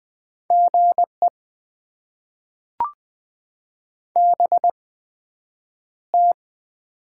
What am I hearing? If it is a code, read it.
Morse code, heard as a keyed tone, about 20 wpm, spaced as for 9 wpm, 700 Hz. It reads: GE BT
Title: Top 20 QSO Elements - 8x Character Spacing - 20wpm